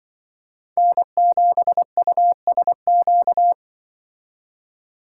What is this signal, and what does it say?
Morse code, heard as a keyed tone, about 23 wpm, 700 Hz. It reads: N7USQ
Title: Speed Racing - USA 1x3 Call Signs 20wpm